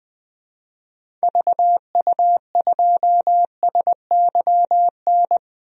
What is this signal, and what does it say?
Morse code, heard as a keyed tone, about 20 wpm, 700 Hz. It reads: VU2SYN
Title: DXCC-100 Call Signs - 6 Characters - 20wpm